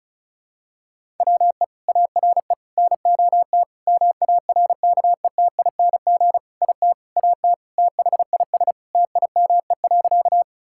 Morse code, heard as a keyed tone, about 36 wpm, 700 Hz. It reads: WE ARE NOT MARKETING IT AT THIS TIME.
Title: Sentences from Top 1000 Words - Review Part 5 - 35wpm